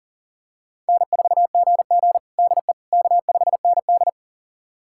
Morse code, heard as a keyed tone, about 40 wpm, 700 Hz. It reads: N4GG DE K5ND
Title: USA Callsign DE Callsign - 4 Characters - 40wpm